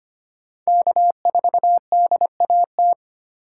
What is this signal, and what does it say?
Morse code, heard as a keyed tone, about 25 wpm, 700 Hz. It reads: K4DAT
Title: USA Call Signs - 5 Characters - 3x Character Spacing - 25wpm